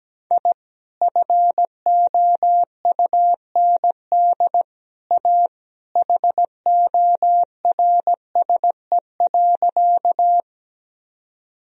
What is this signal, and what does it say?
Morse code, heard as a keyed tone, about 17 wpm, 700 Hz. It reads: I FOUND A HORSE.
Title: Sentences from Top 1000 Words - Review Part 1 - 17wpm